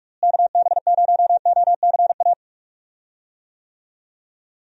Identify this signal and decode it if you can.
Morse code, heard as a keyed tone, about 46 wpm, 700 Hz. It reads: KB0OCA
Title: USA Call Signs - 6 Characters - Repeated 2x - 45wpm